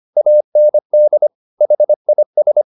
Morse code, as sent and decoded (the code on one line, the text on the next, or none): .- -. -.. / .... .. ...
AND HIS